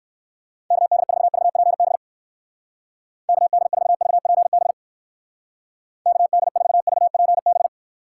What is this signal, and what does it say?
Morse code, heard as a keyed tone, about 59 wpm, 700 Hz. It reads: KD4VPB KD4VPB KD4VPB